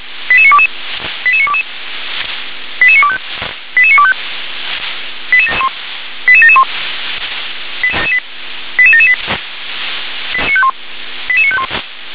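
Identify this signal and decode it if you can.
Five-tone selective call (ZVEI), heard as quick runs of five tones, tones of about 70 ms. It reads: 88922 88922 88925 88925 88771 88778 88951
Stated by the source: ZVEI five-tone selective call 88952 received on SDR.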